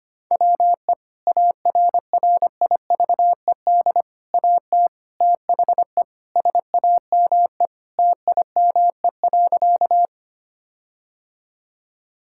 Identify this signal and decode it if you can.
Morse code, heard as a keyed tone, about 25 wpm, 700 Hz. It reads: WE ARRIVED AT THE SAME TIME.
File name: practice / 25wpm / Sentences with 4-Word Phrases - 25wpm